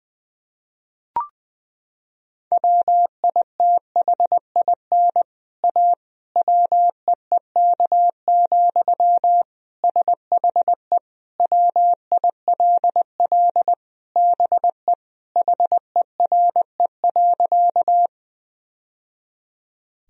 Morse code, heard as a keyed tone, about 19 wpm, 700 Hz. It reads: WITHIN A WEEK, SHE WILL BE HERE.
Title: Sentences from Top 1000 Words - Review Part 4 - 20wpm